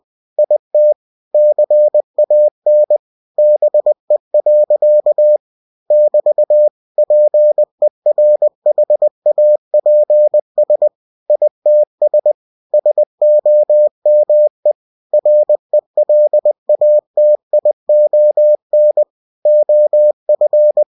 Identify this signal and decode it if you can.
Morse code, heard as a keyed tone, about 20 wpm, 600 Hz. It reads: IT CAN BE. <BT> PERHAPS ITS SOME RELATION OF
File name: cw-books / winnie-the-pooh / winnie-20wpm-page-0074